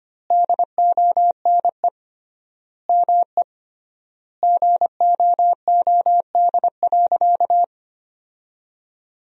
Morse code, heard as a keyed tone, about 25 wpm, 700 Hz. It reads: DONE ME GOOD.
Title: 3x Word-Spacing - Sentences from Top 500 Words - 25wpm